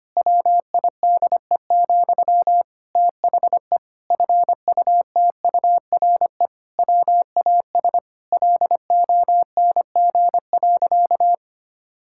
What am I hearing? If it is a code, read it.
Morse code, heard as a keyed tone, about 25 wpm, 700 Hz. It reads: WIDE, THE FUTURE WAS LONG.
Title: Sentences from Top 1000 Words - Review Part 6 - 25wpm